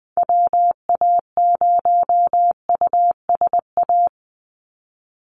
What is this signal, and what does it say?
Morse code, heard as a keyed tone, about 20 wpm, 700 Hz. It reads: WA0USA
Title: Call Signs Repeated 3x - 20wpm